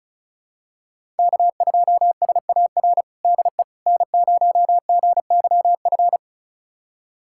Morse code, heard as a keyed tone, about 36 wpm, 700 Hz. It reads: K2SAR DE N0GYF
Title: USA Callsign DE Callsign - 5 Characters - 35wpm